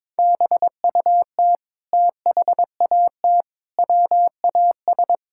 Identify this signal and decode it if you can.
Morse code, heard as a keyed tone, about 21 wpm, 700 Hz. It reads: BUT THAT WAS
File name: practice / 22wpm / Sentences with 8-Word Phrases - 22wpm